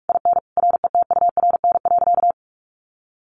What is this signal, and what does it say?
Morse code, heard as a keyed tone, about 44 wpm, 700 Hz. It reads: IN RETURN.
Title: A Sound Of Thunder 45wpm